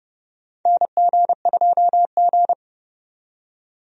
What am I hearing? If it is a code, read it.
Morse code, heard as a keyed tone, about 29 wpm, 700 Hz. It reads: NG2G
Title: USA Call Signs - 4 Characters - 30wpm